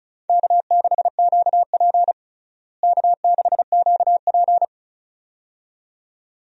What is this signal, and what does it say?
Morse code, heard as a keyed tone, about 34 wpm, 700 Hz. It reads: K6QP K6QP
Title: USA Call Signs - 4 Characters - Repeated 2x - 35wpm